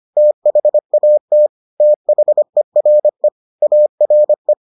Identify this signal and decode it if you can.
Morse code, heard as a keyed tone, about 25 wpm, 600 Hz. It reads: THAT THERE ARE